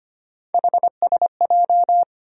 Morse code, heard as a keyed tone, about 25 wpm, 700 Hz. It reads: HSJ
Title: USA Call Sign Postfixes - 3 Letters 25wpm